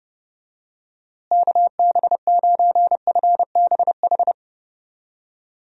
Morse code, heard as a keyed tone, about 30 wpm, 700 Hz. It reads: KB9FBH
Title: USA Call Signs - 6 Characters - Repeated 2x - 30wpm